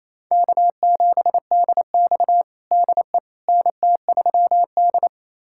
Morse code, heard as a keyed tone, about 28 wpm, 700 Hz. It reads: K7DX DE NT3D
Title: USA Callsign DE Callsign - 4 Characters - 28wpm